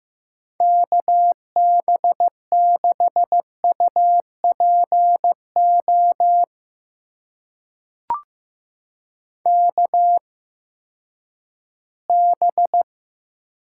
Morse code, heard as a keyed tone, about 15 wpm, spaced as for 10 wpm, 700 Hz. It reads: KB6UPO KB
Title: USA Call Signs - 6 Characters - 8x Character Spacing - 15wpm